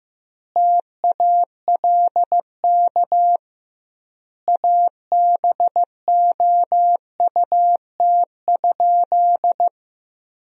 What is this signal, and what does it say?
Morse code, heard as a keyed tone, about 15 wpm, 700 Hz. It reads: TALK ABOUT?
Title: 2x Word-Spacing - Sentences from Top 400 Words - 15wpm